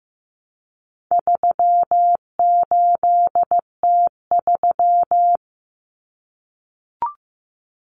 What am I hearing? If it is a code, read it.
Morse code, heard as a keyed tone, about 15 wpm, 700 Hz. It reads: 38T3
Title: Common Cut Numbers - 4-Digit 15wpm